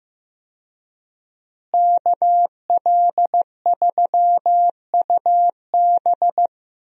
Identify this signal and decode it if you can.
Morse code, heard as a keyed tone, about 15 wpm, 700 Hz. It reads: KL3UB
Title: USA Call Signs - 5 Characters - Repeated 2x - 15wpm